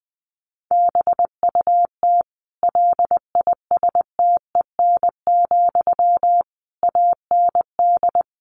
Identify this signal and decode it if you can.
Morse code, heard as a keyed tone, about 20 wpm, 700 Hz. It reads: BUT LISTEN, AND
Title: Sentences from Top 600 Words 20wpm